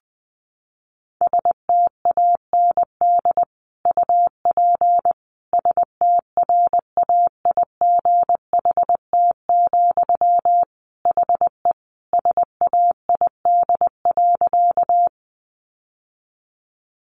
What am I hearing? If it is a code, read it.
Morse code, heard as a keyed tone, about 20 wpm, 700 Hz. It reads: STAND UP STRAIGHT, HE SAID.
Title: Sentences from Top 1000 Words 20wpm